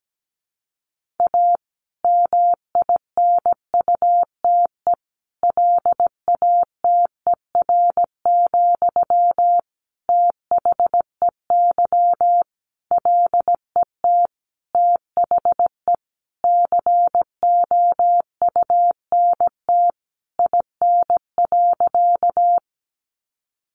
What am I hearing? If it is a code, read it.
Morse code, heard as a keyed tone, about 17 wpm, 700 Hz. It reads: A MINUTE LATER, THEY LET THE COUNT IN.
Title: Sentences from Top 1500 Words 17wpm